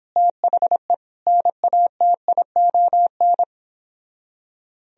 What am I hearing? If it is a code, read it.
Morse code, heard as a keyed tone, about 29 wpm, 700 Hz. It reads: THE NATION